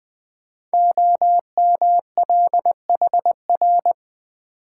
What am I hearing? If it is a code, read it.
Morse code, heard as a keyed tone, about 20 wpm, 700 Hz. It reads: OMLHR
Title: Code Groups - 5 Letters - 20wpm